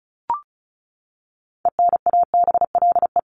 Morse code, heard as a keyed tone, about 34 wpm, 700 Hz. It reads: ENABLE
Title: Top 1251-1500 Words 35wpm